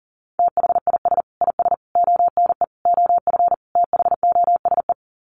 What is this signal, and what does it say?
Morse code, heard as a keyed tone, about 40 wpm, 700 Hz. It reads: THIS IS ONE OF THOSE
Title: Sets of 5 Words Encore2 40wpm